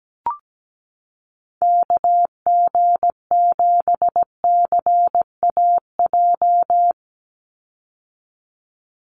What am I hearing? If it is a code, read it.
Morse code, heard as a keyed tone, about 17 wpm, 700 Hz. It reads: KG7CAJ
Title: USA Call Signs 17wpm